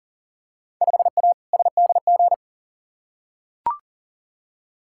Morse code, heard as a keyed tone, about 40 wpm, 700 Hz. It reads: 5A SDG
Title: Field Day Exchanges Encore - 40wpm